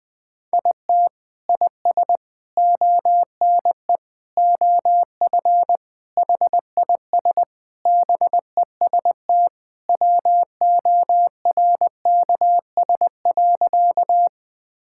Morse code, heard as a keyed tone, about 21 wpm, 700 Hz. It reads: IT IS ONE OF HIS BEST WORKS.
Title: Sentences with 3-Word Phrases - 20wpm